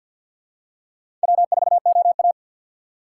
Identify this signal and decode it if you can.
Morse code, heard as a keyed tone, about 50 wpm, 700 Hz. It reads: W4OA
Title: USA Call Signs - 4 Characters - 3x Character Spacing - 50wpm